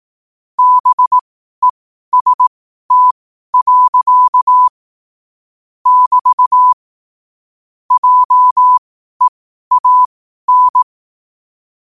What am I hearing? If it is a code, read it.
Morse code, heard as a keyed tone, about 18 wpm, spaced as for 12 wpm, 1000 Hz. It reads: BEST. <BT> JEAN
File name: qotdfiles / 12wpm